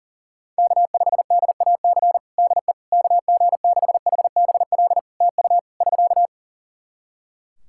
Morse code, heard as a keyed tone, about 37 wpm, 700 Hz. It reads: K5DAC DE KG6HBL TU <SK>
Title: K1USN SST QSO Practice 40wpm